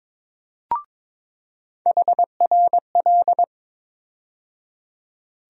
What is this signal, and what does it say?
Morse code, heard as a keyed tone, about 22 wpm, 700 Hz. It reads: HRL
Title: USA Call Sign Postfixes - 3 Letters 22wpm